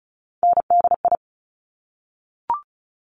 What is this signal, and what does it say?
Morse code, heard as a keyed tone, about 36 wpm, 700 Hz. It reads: NDI
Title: Top Three Letter Combinations in Words 35wpm